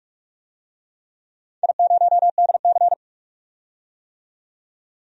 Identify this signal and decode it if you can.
Morse code, heard as a keyed tone, about 44 wpm, 700 Hz. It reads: I0DC
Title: DXCC-100 Call Signs - 4 Characters - 45wpm